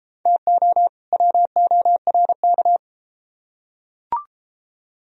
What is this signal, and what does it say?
Morse code, heard as a keyed tone, about 30 wpm, 700 Hz. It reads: TO WORK